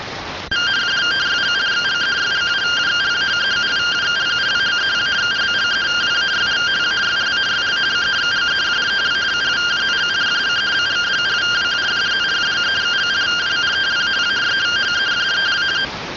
Two-tone teleprinter signal, about 170 Hz shift.